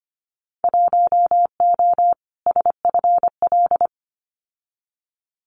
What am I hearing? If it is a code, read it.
Morse code, heard as a keyed tone, about 25 wpm, 700 Hz. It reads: 1O SFL
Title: Winter Field Day Exchanges Encore 25wpm